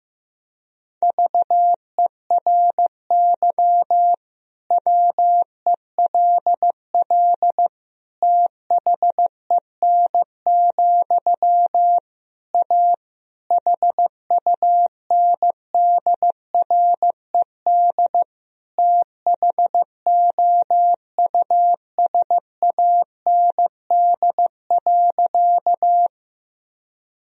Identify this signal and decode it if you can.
Morse code, heard as a keyed tone, about 16 wpm, 700 Hz. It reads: VERY WELL THEN, A HUNDRED THOUSAND.
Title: Sentences from Top 1000 Words - Review Part 3 - 15wpm